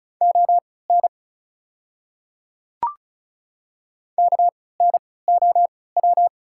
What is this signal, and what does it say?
Morse code, heard as a keyed tone, about 35 wpm, spaced as for 20 wpm, 690 Hz. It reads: ON KNOW
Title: 3x Character Spacing - Top 100 Words - 35wpm